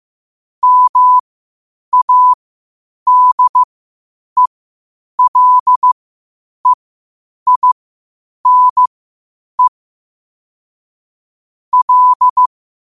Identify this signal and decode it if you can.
Morse code, heard as a keyed tone, about 15 wpm, spaced as for 8 wpm, 1000 Hz. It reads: MADELEINE L
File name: qotdfiles / 8wpm